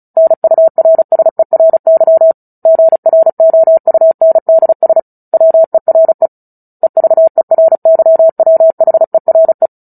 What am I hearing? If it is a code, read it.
Morse code, heard as a keyed tone, about 35 wpm, 650 Hz. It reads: NURSERY GROUNDS WERE EVERYWHERE
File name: Book Two / 09-Chapter 9 - 35 WPM